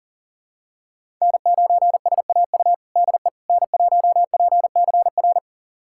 Morse code, heard as a keyed tone, about 40 wpm, 700 Hz. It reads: N9SAU DE N1PCR
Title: USA Callsign DE Callsign - 5 Characters - 40wpm